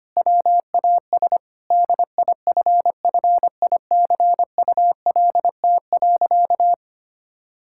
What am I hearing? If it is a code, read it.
Morse code, heard as a keyed tone, about 25 wpm, 700 Hz. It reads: WAS DIFFICULT.
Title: Sentences from Top 1000 Words - Review Part 5 - 25wpm